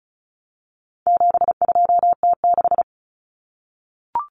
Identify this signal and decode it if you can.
Morse code, heard as a keyed tone, about 34 wpm, 700 Hz. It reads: 72T6
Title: Common Cut Numbers - 4-Digit 35wpm